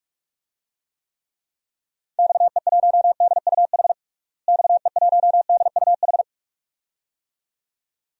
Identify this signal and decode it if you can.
Morse code, heard as a keyed tone, about 48 wpm, 700 Hz. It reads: XE1DUH XE1DUH
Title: DXCC-100 Call Signs - 6 Characters - Repeated 2x - 45wpm